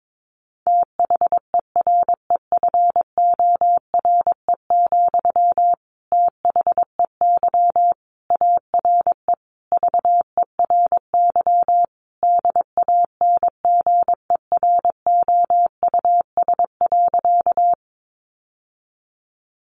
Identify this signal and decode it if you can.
Morse code, heard as a keyed tone, about 22 wpm, 700 Hz. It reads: THEREFORE, THEY ARE VERY DANGEROUS.
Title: Sentences from Top 1200 Words 22wpm